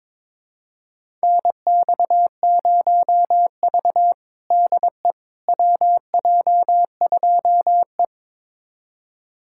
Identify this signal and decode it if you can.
Morse code, heard as a keyed tone, about 21 wpm, 700 Hz. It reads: NX0V DE WJ2E
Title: USA Callsign DE Callsign - 4 Characters - 22wpm